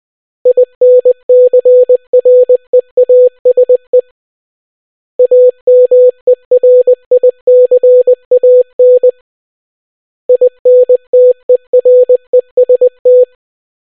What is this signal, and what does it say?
Morse code, heard as a keyed tone, about 20 wpm, 500 Hz. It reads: INCREASE AMERICAN INTEREST